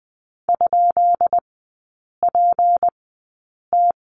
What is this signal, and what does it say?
Morse code, heard as a keyed tone, about 20 wpm, spaced as for 8 wpm, 700 Hz. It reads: ?PT
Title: 152 Lesson 25e - Letter p - Three Characters 20wpm